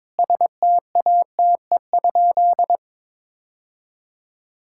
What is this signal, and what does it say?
Morse code, heard as a keyed tone, about 21 wpm, 700 Hz. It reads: STATE?
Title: Sentences from Top 1000 Words - Review Part 5 - 22wpm